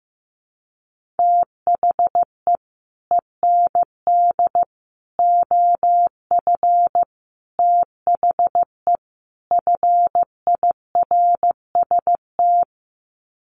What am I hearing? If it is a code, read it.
Morse code, heard as a keyed tone, about 15 wpm, 700 Hz. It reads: THE END OF THE FIRST
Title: Sets of 5 Words Encore 15wpm